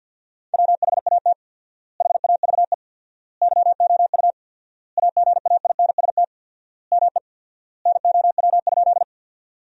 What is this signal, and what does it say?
Morse code, heard as a keyed tone, about 50 wpm, 700 Hz. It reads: WHAT HAVE YOU AGAINST ME NOW?
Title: 4x Word-Spacing - Sentences from Top 400 Words - 50wpm